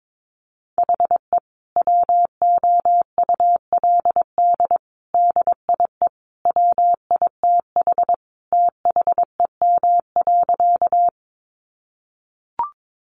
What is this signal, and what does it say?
Morse code, heard as a keyed tone, about 22 wpm, 700 Hz. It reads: HE WOULD DIE WITH THEM.
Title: Sentences from Top 1300 Words 22wpm